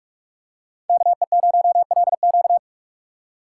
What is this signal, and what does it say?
Morse code, heard as a keyed tone, about 42 wpm, 700 Hz. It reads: KE0LQ
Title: USA Call Signs - 5 Characters - 3x Character Spacing - 45wpm